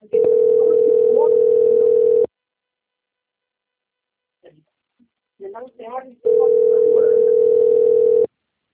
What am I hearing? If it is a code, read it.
Ringback tone (North America), 440 + 480 Hz, with speech.
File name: out- - - - - .69590